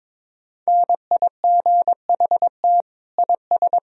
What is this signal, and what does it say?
Morse code, heard as a keyed tone, about 22 wpm, 700 Hz. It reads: NIGHT IS
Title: Sentences from Top 1000 Words - Review Part 1 - 22wpm